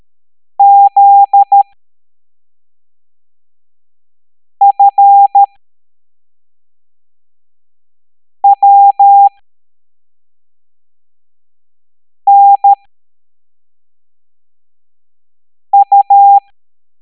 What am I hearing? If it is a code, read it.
Morse code, heard as a keyed tone, about 13 wpm, spaced as for 3 wpm, 800 Hz. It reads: ZFWNU